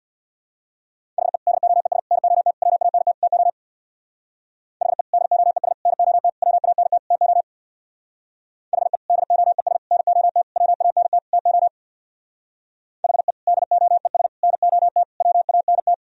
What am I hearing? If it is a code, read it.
Morse code, heard as a keyed tone, about 62 wpm, 700 Hz. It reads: HE DOES NOT WANT TO HE DOES NOT WANT TO HE DOES NOT WANT TO HE DOES NOT WANT
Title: Speed-Racing - Sets of 5 Words 50wpm